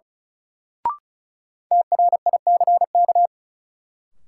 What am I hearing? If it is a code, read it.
Morse code, heard as a keyed tone, about 34 wpm, 700 Hz. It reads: TRICK
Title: Sentences from Top 1500 Words - New Word Drill - 35wpm